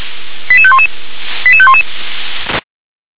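Five-tone selective call (ZVEI), heard as quick runs of five tones, tones of about 70 ms. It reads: 88511 88511
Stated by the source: ZVEI five-tone call received by SDR.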